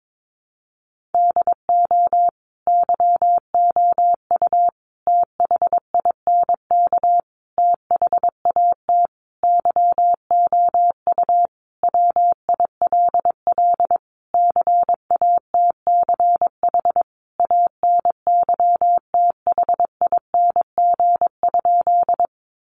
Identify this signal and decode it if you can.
Morse code, heard as a keyed tone, about 22 wpm, 700 Hz. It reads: DO YOU THINK THAT YOU WILL CATCH ANYTHING?
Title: Sentences from Top 1500 Words 22wpm